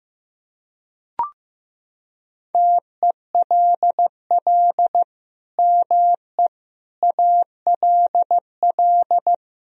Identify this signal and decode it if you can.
Morse code, heard as a keyed tone, about 16 wpm, 700 Hz. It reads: TELL ME ALL
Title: Sentences from Top 1000 Words - Review Part 4 - 15wpm